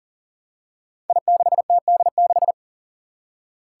Morse code, heard as a keyed tone, about 41 wpm, 700 Hz. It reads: I6TD6
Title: Five Challenging Letter-Number Combinations 40wpm